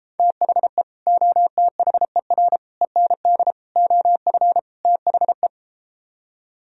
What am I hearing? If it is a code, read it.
Morse code, heard as a keyed tone, about 33 wpm, 700 Hz. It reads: THE OTHER END OF THE